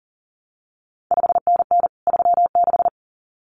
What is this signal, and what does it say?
Morse code, heard as a keyed tone, about 40 wpm, 700 Hz. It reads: 5NN 36
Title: CQ WW DX Contest Exchange Encore - 40wpm